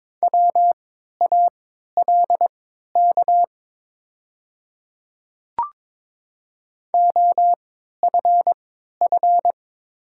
Morse code, heard as a keyed tone, about 22 wpm, spaced as for 13 wpm, 700 Hz. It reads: WALK OFF
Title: 3x Character Spacing - Top 300 Words - 22wpm